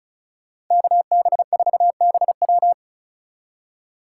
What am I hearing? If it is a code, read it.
Morse code, heard as a keyed tone, about 47 wpm, 700 Hz. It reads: KB4BW